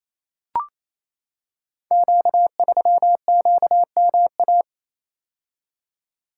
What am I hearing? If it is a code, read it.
Morse code, heard as a keyed tone, about 27 wpm, 690 Hz. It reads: Q3QMA
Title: Code Groups - 5 Letter:Numbers - 28wpm